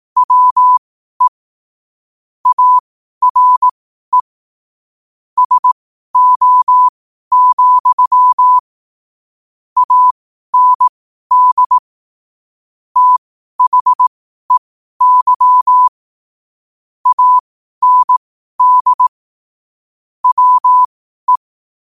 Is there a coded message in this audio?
Yes: Morse code from 0.2 to 21.4 s, heard as a keyed tone, about 18 wpm, spaced as for 12 wpm, 1000 Hz. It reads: WE ARE SO, AND THEY AND WE